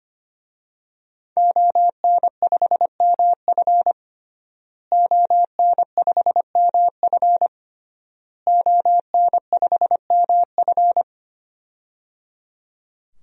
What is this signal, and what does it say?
Morse code, heard as a keyed tone, about 25 wpm, 700 Hz. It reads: ON5MF ON5MF ON5MF
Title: DXCC-100 Call Signs - 5 Characters - Repeated 3x - 25wpm